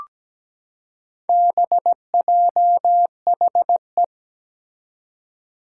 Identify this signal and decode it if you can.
Morse code, heard as a keyed tone, about 17 wpm, 700 Hz. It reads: BJHE
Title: Four Challenging Letter-Number Combinations 17wpm